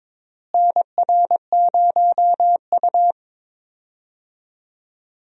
Morse code, heard as a keyed tone, about 22 wpm, 700 Hz. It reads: NR0U
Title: USA Call Signs - 4 Characters - Repeated 3x - 22wpm